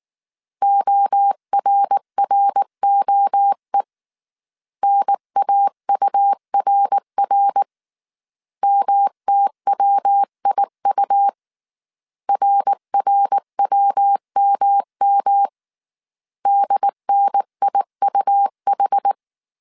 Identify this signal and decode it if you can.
Morse code, heard as a keyed tone, about 19 wpm, 800 Hz. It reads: ORROE NAURR MTWIU RRWMM DNIUH
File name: CW / DET / 1 3 1500C S B 18wpm 1916